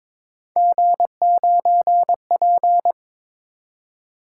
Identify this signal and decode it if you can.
Morse code, heard as a keyed tone, about 26 wpm, 700 Hz. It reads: G9P
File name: Speed-Racing - Three Letter-Number Combinations - 22wpm